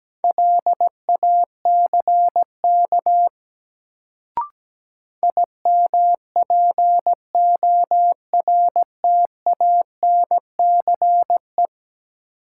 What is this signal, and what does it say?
Morse code, heard as a keyed tone, about 17 wpm, 700 Hz. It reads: LACK IMPORTANCE
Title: Sets of 3 Words - New Word Drill - 17wpm